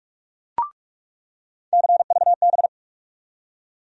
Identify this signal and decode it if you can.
Morse code, heard as a keyed tone, about 46 wpm, 700 Hz. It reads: CVB
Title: USA Call Sign Postfixes - 3 Letters 45wpm